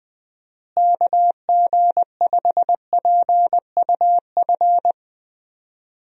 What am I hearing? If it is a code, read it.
Morse code, heard as a keyed tone, about 21 wpm, 700 Hz. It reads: KG5PUF
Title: USA Call Signs - 6 Characters - Repeated 2x - 20wpm